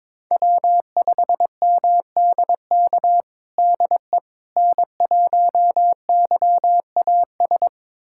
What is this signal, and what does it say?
Morse code, heard as a keyed tone, about 22 wpm, 700 Hz. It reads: W5MDK DE N1YAS
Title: USA Callsign DE Callsign - 5 Characters - 22wpm